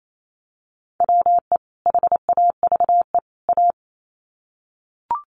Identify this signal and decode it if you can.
Morse code, heard as a keyed tone, about 28 wpm, 700 Hz. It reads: WE HAVE A
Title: Sets of 3 Words 28wpm